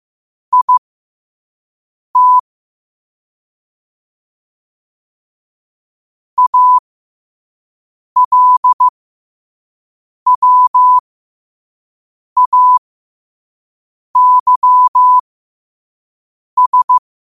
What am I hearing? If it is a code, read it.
Morse code, heard as a keyed tone, about 15 wpm, spaced as for 5 wpm, 1000 Hz. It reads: IT ALWAYS
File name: qotdfiles / 5wpm